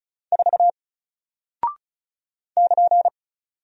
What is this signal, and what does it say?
Morse code, heard as a keyed tone, about 35 wpm, 700 Hz. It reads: 4 <KN>
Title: Single Character (Letter:Number:Punctuation:Prosign) 35wpm